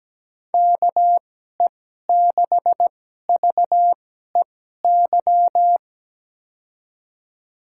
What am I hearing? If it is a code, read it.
Morse code, heard as a keyed tone, about 17 wpm, spaced as for 12 wpm, 700 Hz. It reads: KE6VEY
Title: USA Call Signs - 2x Character Spacing - 17wpm